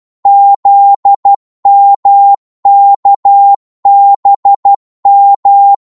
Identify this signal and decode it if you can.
Morse code, heard as a keyed tone, about 12 wpm, 800 Hz. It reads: ZMKBM
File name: Morse / ltr-12wpm